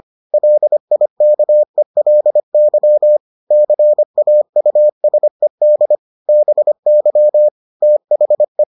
Morse code, heard as a keyed tone, about 25 wpm, 600 Hz. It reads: LIKELY CAUSED BY THE